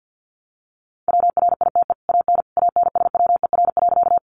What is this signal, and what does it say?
Morse code, heard as a keyed tone, about 50 wpm, 700 Hz. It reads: WRITE AN ANSWER.
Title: Sentences from Top 300 Words 50wpm